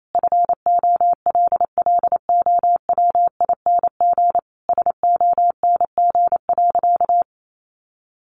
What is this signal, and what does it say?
Morse code, heard as a keyed tone, about 28 wpm, 700 Hz. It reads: FOLLOWING SONG.
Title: Sentences from Top 900 Words 28wpm